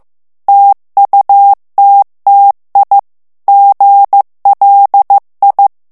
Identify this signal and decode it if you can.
Morse code, heard as a keyed tone, about 15 wpm, 800 Hz. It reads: TUTTI GLI
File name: ITALIEN texte clair 1 vitesse 14 mots minute